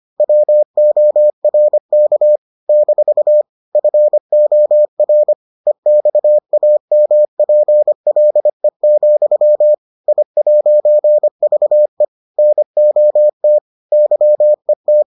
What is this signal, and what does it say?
Morse code, heard as a keyed tone, about 25 wpm, 600 Hz. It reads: WORK - FOR EXAMPLE, I'VE NOT YET